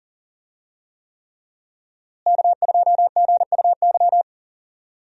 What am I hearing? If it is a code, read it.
Morse code, heard as a keyed tone, about 40 wpm, 700 Hz. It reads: K2GUY
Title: USA Call Signs - 5 Characters - Repeated 3x - 40wpm